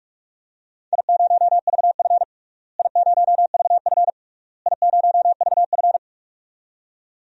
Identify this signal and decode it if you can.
Morse code, heard as a keyed tone, about 46 wpm, 700 Hz. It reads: I0VF I0VF I0VF
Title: DXCC-100 Call Signs - 4 Characters - Repeated 3x - 45wpm